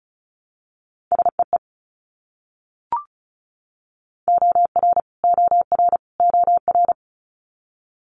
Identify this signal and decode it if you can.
Morse code, heard as a keyed tone, about 36 wpm, 700 Hz. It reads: SEE OR OR OR
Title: Multi-letter Phonemes Repeated 3x 35wpm